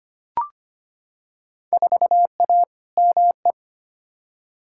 Morse code, heard as a keyed tone, about 25 wpm, 700 Hz. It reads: 4A ME
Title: Field Day Exchanges - 25wpm